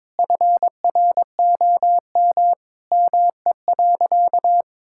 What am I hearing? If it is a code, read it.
Morse code, heard as a keyed tone, about 22 wpm, 700 Hz. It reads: FROM ME.
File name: Sentences with 7-Word Phrases - 22wpm